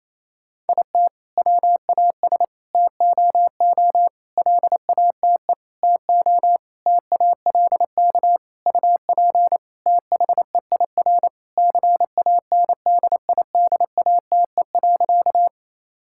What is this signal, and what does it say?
Morse code, heard as a keyed tone, about 28 wpm, 700 Hz. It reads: IT WAS TOO LATE TO TALK UP THEIR CANDIDATE.
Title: Sentences from Top 1600 Words 28wpm